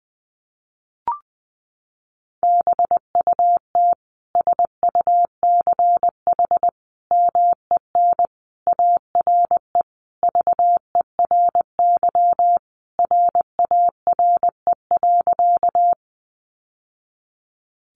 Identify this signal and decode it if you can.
Morse code, heard as a keyed tone, about 20 wpm, 700 Hz. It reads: BUT SUCH MEN ARE VERY RARE.
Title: Sentences from Top 1100 Words 20wpm